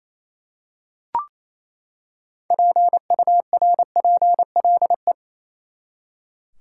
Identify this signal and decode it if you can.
Morse code, heard as a keyed tone, about 28 wpm, 700 Hz. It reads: PURPLE
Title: Six Letter Words - 28wpm